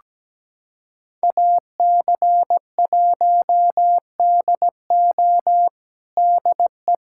Morse code, heard as a keyed tone, about 17 wpm, 700 Hz. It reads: AC1DO DE
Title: USA Callsign DE Callsign - 5 Characters - 17wpm